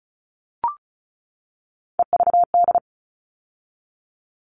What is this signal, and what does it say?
Morse code, heard as a keyed tone, about 34 wpm, 700 Hz. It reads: EVD
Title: All Cut Numbers - 3-Digit 35wpm